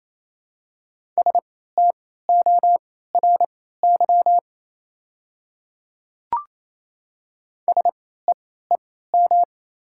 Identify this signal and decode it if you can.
Morse code, heard as a keyed tone, about 28 wpm, spaced as for 16 wpm, 700 Hz. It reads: STORY SEEM
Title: 3x Character Spacing - Top 300 Words - 28wpm